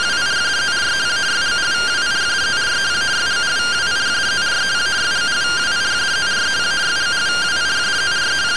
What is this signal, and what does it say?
Two-tone teleprinter signal, about 170 Hz shift.